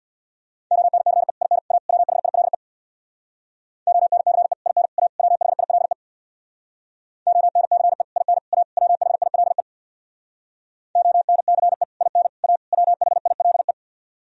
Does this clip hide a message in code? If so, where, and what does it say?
Yes: Morse code from 0.7 to 13.7 s, heard as a keyed tone, about 64 wpm, 700 Hz. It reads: ONCE IN A WHILE ONCE IN A WHILE ONCE IN A WHILE ONCE IN A WHILE